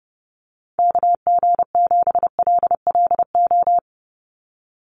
Morse code, heard as a keyed tone, about 30 wpm, 700 Hz. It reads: KG7LLO
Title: USA Call Signs 30wpm